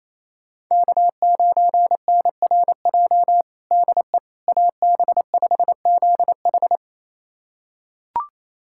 Morse code, heard as a keyed tone, about 28 wpm, 700 Hz. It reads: K9NRJ DE AB5ZH
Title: USA Callsign DE Callsign - 5 Characters - 28wpm